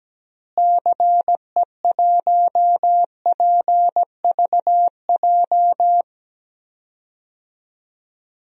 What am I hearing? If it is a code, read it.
Morse code, heard as a keyed tone, about 17 wpm, 700 Hz. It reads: CE1PVJ
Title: DXCC-100 Call Signs - 6 Characters - Repeated 3x - 17wpm